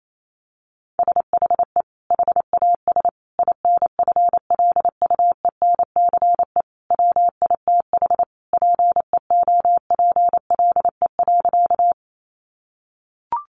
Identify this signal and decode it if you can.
Morse code, heard as a keyed tone, about 28 wpm, 700 Hz. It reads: SHE HAS INFLUENCE WITH PEOPLE.
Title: Sentences from Top 1100 Words 28wpm